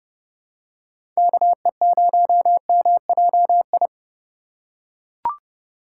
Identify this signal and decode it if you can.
Morse code, heard as a keyed tone, about 30 wpm, 700 Hz. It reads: KE0MJI
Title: USA Call Signs - 6 Characters - 2x Character Spacing - 30wpm